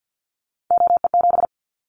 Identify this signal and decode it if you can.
Morse code, heard as a keyed tone, about 50 wpm, 700 Hz. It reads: OE7
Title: Callsign Prefixes for Top 50 DXCC - 50wpm